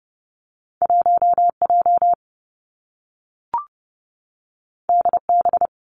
Morse code, heard as a keyed tone, about 30 wpm, 700 Hz. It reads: 1 J D B
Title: Two Challenging Letter-Number Combinations 30wpm